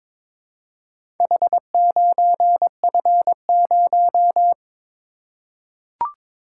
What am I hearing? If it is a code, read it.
Morse code, heard as a keyed tone, about 22 wpm, 700 Hz. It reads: H9F0
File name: practice / 22wpm / Four Letter-Number Combinations - 22wpm